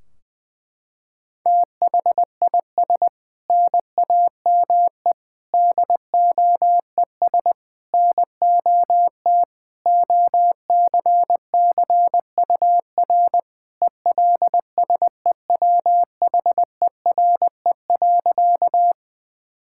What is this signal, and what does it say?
Morse code, heard as a keyed tone, about 21 wpm, 700 Hz. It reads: THIS NAME DOES NOT OCCUR ELSEWHERE.
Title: Sentences from Top 1600 Words 20wpm